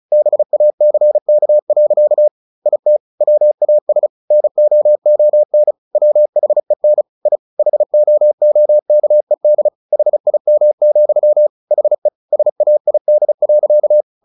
Morse code, heard as a keyed tone, about 34 wpm, 600 Hz. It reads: BACK. IT WAS NOON WHEN I HOOKED HIM, HE SAID.
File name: the old man and the sea / the old man and the sea 1 35wpm